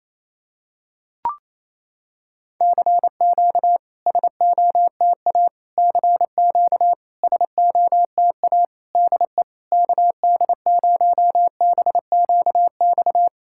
Morse code, heard as a keyed tone, about 28 wpm, 700 Hz. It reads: CQ SOTA CQ SOTA DE KD0BQX
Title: SOTA QSO Practice 28wpm